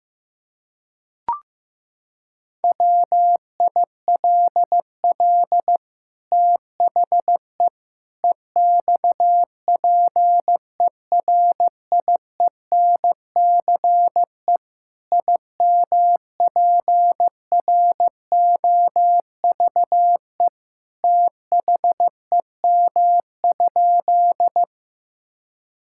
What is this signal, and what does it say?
Morse code, heard as a keyed tone, about 15 wpm, 700 Hz. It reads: WILL THE EXPERIENCE IMPROVE THEM?
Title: Sentences from Top 1000 Words - Review Part 5 - 15wpm